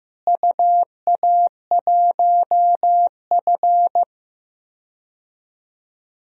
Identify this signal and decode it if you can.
Morse code, heard as a keyed tone, about 16 wpm, 700 Hz. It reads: UA1F
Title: DXCC-100 Call Signs - 4 Characters - Repeated 2x - 15wpm